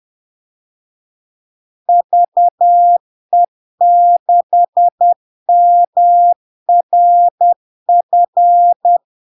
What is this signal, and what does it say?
Morse code, heard as a keyed tone, about 10 wpm, 700 Hz. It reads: VE6MRF
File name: LARK-FD-2022-Log-in-CW-10wpm-700-Hz